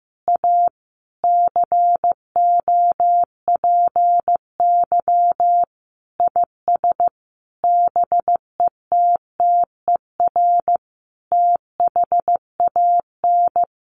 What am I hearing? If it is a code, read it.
Morse code, heard as a keyed tone, about 15 wpm, 700 Hz. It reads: A COPY IS BETTER THAN
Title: Sentences from Top 1400 Words 15wpm